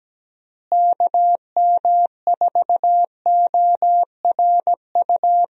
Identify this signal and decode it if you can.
Morse code, heard as a keyed tone, about 17 wpm, 700 Hz. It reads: KM4ORU
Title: USA Call Signs - 6 Characters - Repeated 2x - 17wpm